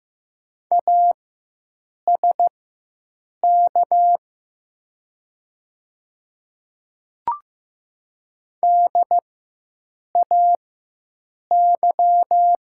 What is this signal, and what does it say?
Morse code, heard as a keyed tone, about 15 wpm, spaced as for 7 wpm, 700 Hz. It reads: ASK DAY
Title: 4x Character Spacing - Top 200 Words - 15wpm